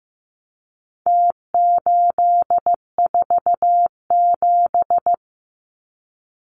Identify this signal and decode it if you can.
Morse code, heard as a keyed tone, about 15 wpm, 700 Hz. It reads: T847
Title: Numbers - Contest Style - 4-Digit 15wpm